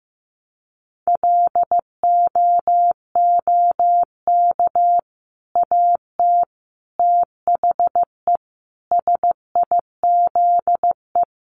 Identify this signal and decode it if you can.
Morse code, heard as a keyed tone, about 15 wpm, 700 Hz. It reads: LOOK AT THE SIZE
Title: Sentences from Top 1200 Words 15wpm